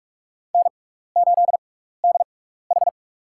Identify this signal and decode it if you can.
Morse code, heard as a keyed tone, about 43 wpm, spaced as for 15 wpm, 700 Hz. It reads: N8DH
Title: USA Call Signs - 6x Character Spacing - 45wpm